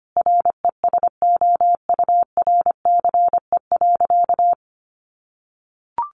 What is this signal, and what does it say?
Morse code, heard as a keyed tone, about 25 wpm, 700 Hz. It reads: RESOURCE.
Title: Sentences from Top 1300 Words 25wpm